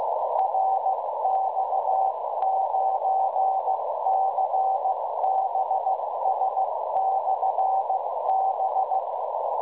Morse code, heard as a keyed tone, about 16 wpm, 770 Hz, with noise. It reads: RX9CBS/B
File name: fr / websdr recordin